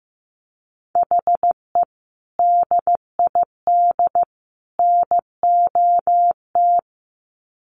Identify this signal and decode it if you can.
Morse code, heard as a keyed tone, about 15 wpm, 700 Hz. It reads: HE DID NOT
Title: Sets of 3 Words 15wpm